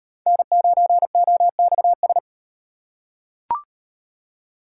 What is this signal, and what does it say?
Morse code, heard as a keyed tone, about 36 wpm, 700 Hz. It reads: N9OXS